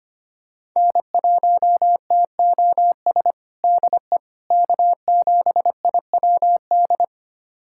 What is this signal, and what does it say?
Morse code, heard as a keyed tone, about 25 wpm, 700 Hz. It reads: N1TOS DE K7IWD
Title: USA Callsign DE Callsign - 5 Characters - 25wpm